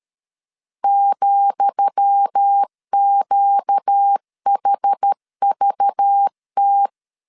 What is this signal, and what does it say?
Morse code, heard as a keyed tone, about 13 wpm, 800 Hz. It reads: ,QHVT
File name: LEZIONI / 10 SIGNS / 10D COMA 12wpm 2544